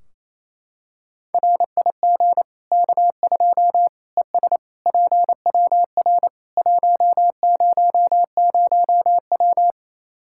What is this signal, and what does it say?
Morse code, heard as a keyed tone, about 27 wpm, 700 Hz. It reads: RIG K2 ES PWR 100W
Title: K1USN SST QSO Practice 28wpm